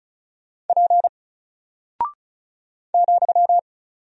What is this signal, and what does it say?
Morse code, heard as a keyed tone, about 35 wpm, 700 Hz. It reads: P ,
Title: Single Character (Letter:Number:Punctuation:Prosign) 35wpm